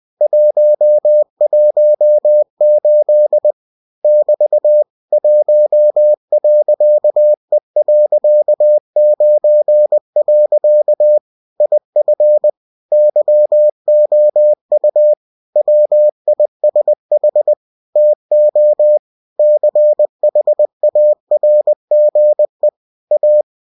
Morse code, heard as a keyed tone, about 20 wpm, 600 Hz. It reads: 118 <BT> 1.E.9. IF YOU WISH TO CHARGE A